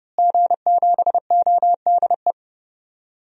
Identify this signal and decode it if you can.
Morse code, heard as a keyed tone, about 30 wpm, 700 Hz. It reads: G7ODE
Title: DXCC-100 Call Signs - 5 Characters - Repeated 2x - 30wpm